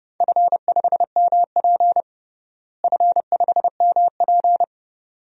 Morse code, heard as a keyed tone, about 30 wpm, 700 Hz. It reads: F5MP F5MP
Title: DXCC-100 Call Signs - 4 Characters - Repeated 2x - 30wpm